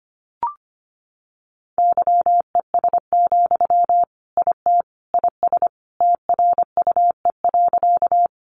Morse code, heard as a keyed tone, about 25 wpm, 700 Hz. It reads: YES, IT IS TRUE.
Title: Sentences from Top 500 Words 25wpm